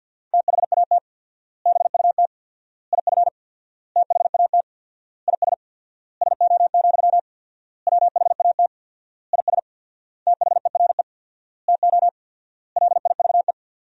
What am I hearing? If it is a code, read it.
Morse code, heard as a keyed tone, about 53 wpm, 700 Hz. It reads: THAT BUT IF THAT IS SO, WHAT IS THERE TO LIVE
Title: 4x Word-Spacing - Sentences from Top 200 Words - 50wpm